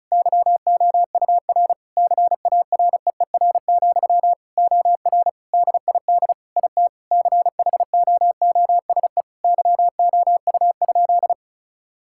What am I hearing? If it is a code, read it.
Morse code, heard as a keyed tone, about 36 wpm, 700 Hz. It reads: YOUR CAREER, OR DID IT CHOOSE YOU?
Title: Sentences from Top 1000 Words - Review Part 6 - 35wpm